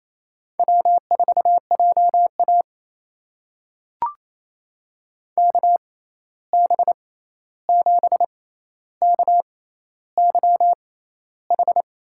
Morse code, heard as a keyed tone, about 27 wpm, spaced as for 13 wpm, 700 Hz. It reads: W4JA KB7KYH
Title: USA Call Signs - 6x Character Spacing - 28wpm